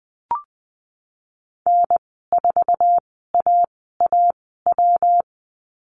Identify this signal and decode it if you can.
Morse code, heard as a keyed tone, about 20 wpm, spaced as for 15 wpm, 700 Hz. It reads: N4AAW
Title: 128 Lesson 21g - Number 6 - Call Signs 20wpm